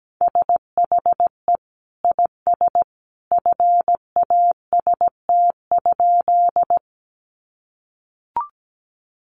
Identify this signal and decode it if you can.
Morse code, heard as a keyed tone, about 17 wpm, 700 Hz. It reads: SHE IS FAST?
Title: Sentences from Top 400 Words 17wpm